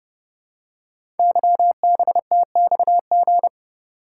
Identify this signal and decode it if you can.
Morse code, heard as a keyed tone, about 28 wpm, 700 Hz. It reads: YBTXG
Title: Code Groups - 5 Letters - 30wpm